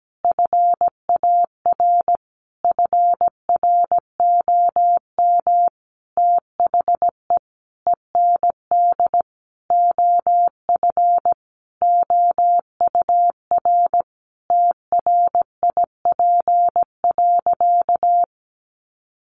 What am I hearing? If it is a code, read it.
Morse code, heard as a keyed tone, about 17 wpm, 700 Hz. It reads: FAR FROM THE END OF OUR TRIP.
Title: Sentences from Top 1300 Words 17wpm